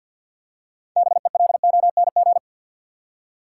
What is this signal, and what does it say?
Morse code, heard as a keyed tone, about 53 wpm, 700 Hz. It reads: BELONG
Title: Six Letter Words - 50wpm